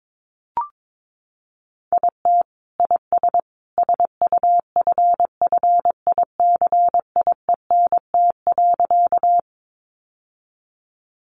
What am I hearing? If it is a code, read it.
Morse code, heard as a keyed tone, about 22 wpm, 700 Hz. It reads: IT IS SUFFICIENT.
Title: Sentences from Top 1400 Words 22wpm